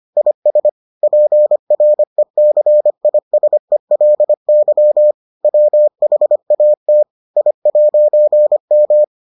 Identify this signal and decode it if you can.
Morse code, heard as a keyed tone, about 25 wpm, 600 Hz. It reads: IS PRECISELY WHAT I'M